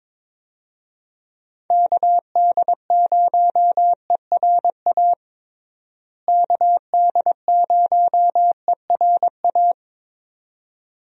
Morse code, heard as a keyed tone, about 22 wpm, 700 Hz. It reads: KD0ERA KD0ERA
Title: USA Call Signs - 6 Characters - Repeated 2x - 22wpm